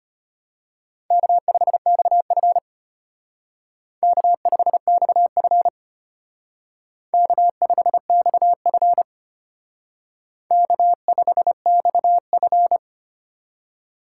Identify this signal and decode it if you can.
Morse code, heard as a keyed tone, about 29 wpm, 700 Hz. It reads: K5XF K5XF K5XF K5XF
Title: Speed Racing - USA 1x2 Call Signs 25wpm